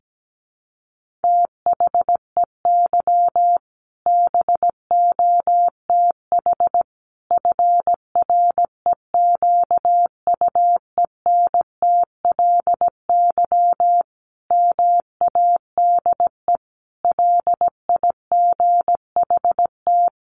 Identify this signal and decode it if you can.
Morse code, heard as a keyed tone, about 17 wpm, 700 Hz. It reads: THEY BOTH FREQUENTLY MADE LIGHT
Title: Sentences from Top 1300 Words 17wpm